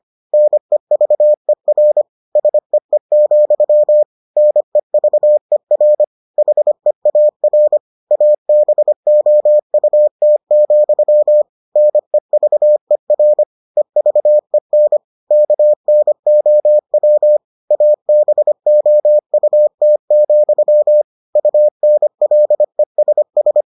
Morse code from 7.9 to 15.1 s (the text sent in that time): ABOUT, NEVER EVEN